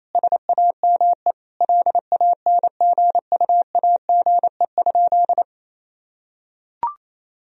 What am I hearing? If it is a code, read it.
Morse code, heard as a keyed tone, about 27 wpm, 700 Hz. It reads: SAME LANGUAGE?
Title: Sentences from Top 1000 Words - Review Part 3 - 28wpm